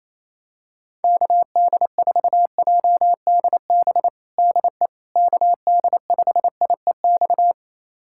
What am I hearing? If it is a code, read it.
Morse code, heard as a keyed tone, about 28 wpm, 700 Hz. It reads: KD4JDB DE KD5IEX
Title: USA Callsign DE Callsign 28wpm